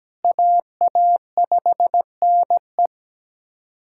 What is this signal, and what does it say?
Morse code, heard as a keyed tone, about 17 wpm, 700 Hz. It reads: AA5NE
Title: USA Callsign DE Callsign - 5 Characters - 17wpm